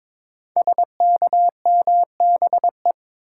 Morse code, heard as a keyed tone, about 22 wpm, 700 Hz. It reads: SKMBE
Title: Code Groups - 5 Letters - 22wpm